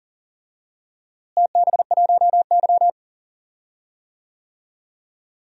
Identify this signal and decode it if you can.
Morse code, heard as a keyed tone, about 40 wpm, 700 Hz. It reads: TB1Y
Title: DXCC-100 Call Signs - 4 Characters - Repeated 3x - 40wpm